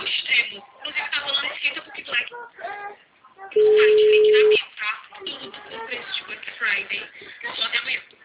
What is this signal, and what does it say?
Ringback tone (Europe, 425 Hz), with speech.